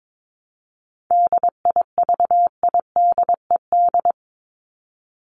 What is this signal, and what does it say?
Morse code, heard as a keyed tone, about 22 wpm, 700 Hz. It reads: DIVIDED
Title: ED Suffix - Top 100 Words with Base Words 22wpm